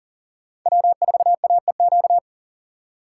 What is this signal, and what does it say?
Morse code, heard as a keyed tone, about 40 wpm, 700 Hz. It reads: W4AEQ
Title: USA Call Signs - 5 Characters - 2x Character Spacing - 40wpm